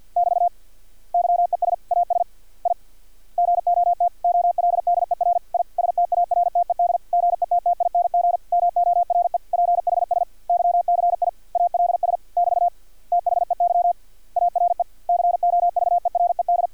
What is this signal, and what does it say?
Morse code, heard as a keyed tone, about 50 wpm, 700 Hz. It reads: <BT> YES AS I GOT OLDER I STARTED GETTING MORE PHSYCIALS <BT> THEY ARE COVERED
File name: archived / CRSnet